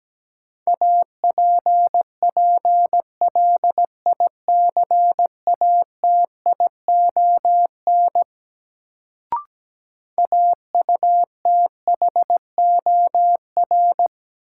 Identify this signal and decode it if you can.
Morse code, heard as a keyed tone, about 17 wpm, 700 Hz. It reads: APPLICATION AUTHOR
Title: Sentences from Top 1200 Words - New Word Drill Alt - 17wpm